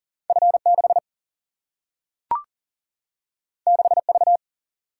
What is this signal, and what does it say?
Morse code, heard as a keyed tone, about 40 wpm, 700 Hz. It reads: F6 6V
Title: Single-Letter Single-Number 40wpm